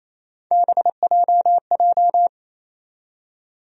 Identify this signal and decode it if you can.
Morse code, heard as a keyed tone, about 28 wpm, 700 Hz. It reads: BJJ
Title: USA Call Sign Postfixes - 3 Letters 28wpm